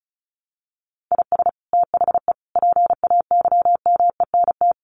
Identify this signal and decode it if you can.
Morse code, heard as a keyed tone, about 34 wpm, 700 Hz. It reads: IS THE PAYMENT
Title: Sentences from Top 1500 Words 35wpm